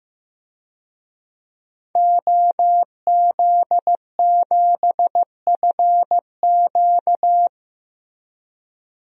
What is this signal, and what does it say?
Morse code, heard as a keyed tone, about 15 wpm, 700 Hz. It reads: OZ7FQ
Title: DXCC-100 Call Signs - 5 Characters - Repeated 2x - 15wpm